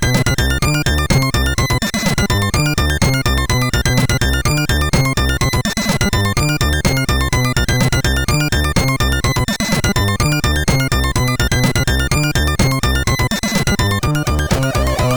There is no code.